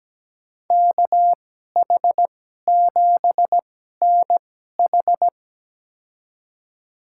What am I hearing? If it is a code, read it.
Morse code, heard as a keyed tone, about 17 wpm, spaced as for 12 wpm, 700 Hz. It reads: KH7NH
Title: USA Call Signs - 5 Characters - 2x Character Spacing - 17wpm